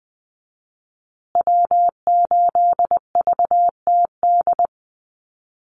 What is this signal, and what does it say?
Morse code, heard as a keyed tone, about 20 wpm, 700 Hz. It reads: W8VTD
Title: USA Call Signs Repeated 3x 20wpm